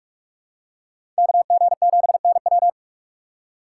Morse code, heard as a keyed tone, about 44 wpm, 700 Hz. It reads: KG7NW
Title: USA Call Signs - 2x Character Spacing - 45wpm